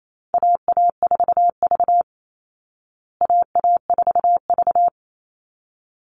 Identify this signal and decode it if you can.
Morse code, heard as a keyed tone, about 28 wpm, 700 Hz. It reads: AA4V AA4V
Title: Call Signs Repeated 3x - 28wpm